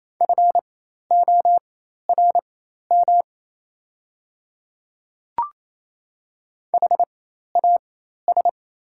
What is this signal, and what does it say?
Morse code, heard as a keyed tone, about 28 wpm, spaced as for 13 wpm, 700 Hz. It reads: FORM HAS
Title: 4x Character Spacing - Top 200 Words - 28wpm